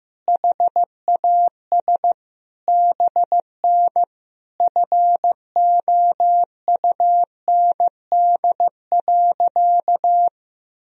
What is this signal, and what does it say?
Morse code, heard as a keyed tone, about 15 wpm, 700 Hz. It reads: HAS BN FOUND.
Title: Sentences from Top 300 Words and Abbreviations 15wpm